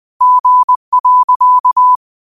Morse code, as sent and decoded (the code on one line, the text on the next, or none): --. .-.-.-
G.